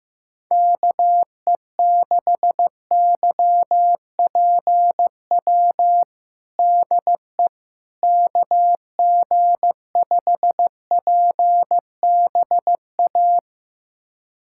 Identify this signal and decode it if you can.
Morse code, heard as a keyed tone, about 15 wpm, 700 Hz. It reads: KE6YPW DE KG5PBA
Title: USA Callsign DE Callsign - 6 Characters - 15wpm